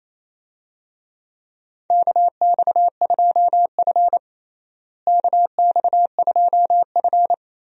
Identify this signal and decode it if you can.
Morse code, heard as a keyed tone, about 28 wpm, 700 Hz. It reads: KX2F KX2F
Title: USA Call Signs - 4 Characters - Repeated 2x - 28wpm